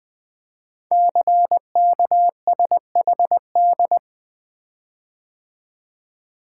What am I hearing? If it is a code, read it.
Morse code, heard as a keyed tone, about 20 wpm, 700 Hz. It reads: CKSHD
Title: Code Groups - 5 Letters - 20wpm